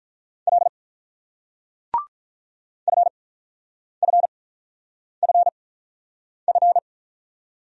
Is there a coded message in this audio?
Yes: Morse code from 0.5 to 6.8 s, heard as a keyed tone, about 48 wpm, spaced as for 9 wpm, 700 Hz. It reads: L FFFF